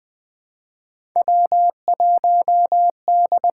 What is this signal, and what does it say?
Morse code, heard as a keyed tone, about 20 wpm, 700 Hz. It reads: W1D
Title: USA Call Signs - 5 Characters - 6x Character Spacing - 20wpm